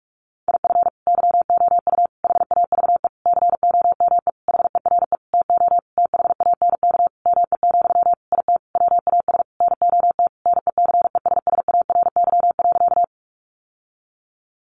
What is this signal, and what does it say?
Morse code, heard as a keyed tone, about 44 wpm, 700 Hz. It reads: IF YOU HAVE COME HERE TO THANK ME, IT WAS NOT NECESSARY.
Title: Sentences from Top 800 Words 45wpm